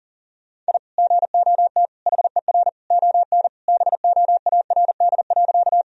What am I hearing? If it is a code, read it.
Morse code, heard as a keyed tone, about 40 wpm, 700 Hz. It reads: I GOT HER ON BOARD.
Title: Sentences from Top 1000 Words - Review Part 4 - 40wpm